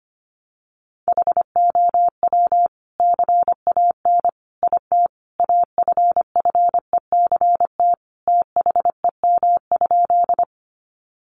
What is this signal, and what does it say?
Morse code, heard as a keyed tone, about 25 wpm, 700 Hz. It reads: HOW CAN IT AFFECT THEM?
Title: Sentences from Top 900 Words 25wpm